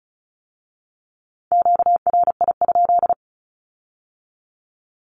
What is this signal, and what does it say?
Morse code, heard as a keyed tone, about 34 wpm, 700 Hz. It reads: QRI?
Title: Q-Codes with Spoken Meaning - 35wpm